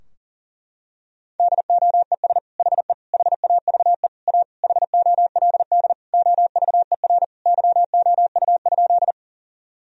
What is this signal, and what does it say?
Morse code, heard as a keyed tone, about 37 wpm, 700 Hz. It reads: DOES HE HAVE A HOLD OVER YOU?
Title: Sentences from Top 1000 Words - Review Part 2 - 40wpm